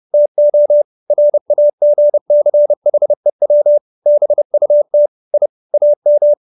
Morse code, heard as a keyed tone, about 30 wpm, 590 Hz. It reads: TO RAGCHEW BUT I AM
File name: QSO315 30